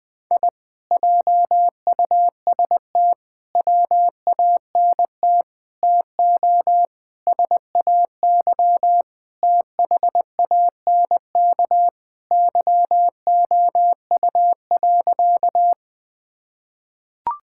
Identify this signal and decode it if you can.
Morse code, heard as a keyed tone, about 20 wpm, 700 Hz. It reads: I JUST WANT TO SAY THANK YOU.
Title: Sentences with 2-Word Phrases - Spoken Beginning - 20wpm